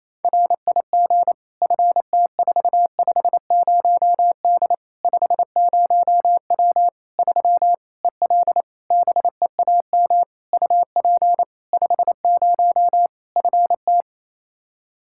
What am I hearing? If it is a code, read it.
Morse code, heard as a keyed tone, about 28 wpm, 700 Hz. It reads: RIG FT450D 50W 3 EL BEAM UP 50 FT
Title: K1USN SST QSO Practice 28wpm